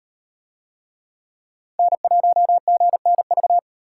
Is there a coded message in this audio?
Yes: Morse code from 1.8 to 3.6 s, heard as a keyed tone, about 37 wpm, 700 Hz. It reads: N1GNV